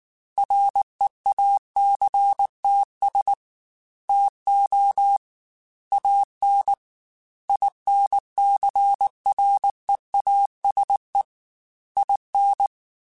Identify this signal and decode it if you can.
Morse code, heard as a keyed tone, about 19 wpm, 800 Hz. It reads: REACTS TO AN INCREASE IN